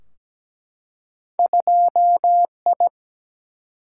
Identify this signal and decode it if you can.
Morse code, heard as a keyed tone, about 17 wpm, 700 Hz. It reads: 2I
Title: Single-Letter Single-Number 17wpm